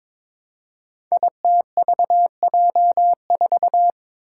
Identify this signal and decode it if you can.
Morse code, heard as a keyed tone, about 22 wpm, 700 Hz. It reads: ITVJ4
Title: Five Challenging Letter-Number Combinations 22wpm